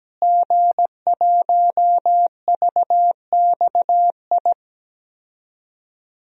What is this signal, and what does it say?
Morse code, heard as a keyed tone, about 17 wpm, 700 Hz. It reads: G1VXI
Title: DXCC-100 Call Signs - 5 Characters - Repeated 3x - 17wpm